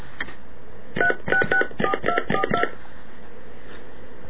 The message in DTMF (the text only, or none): 3664242